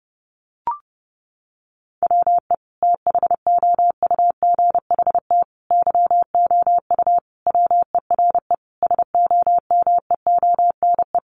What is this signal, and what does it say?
Morse code, heard as a keyed tone, about 30 wpm, 700 Hz. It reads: WE THOUGHT YOU WERE SOMEONE
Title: Sentences from Top 800 Words 30wpm